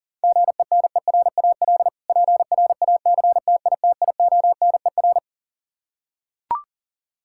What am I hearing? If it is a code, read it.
Morse code, heard as a keyed tone, about 37 wpm, 700 Hz. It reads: GENERAL PRACTITIONER
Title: K1USN SST QSO Practice 40wpm